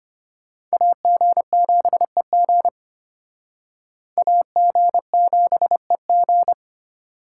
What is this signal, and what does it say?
Morse code, heard as a keyed tone, about 32 wpm, 700 Hz. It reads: AG7EG AG7EG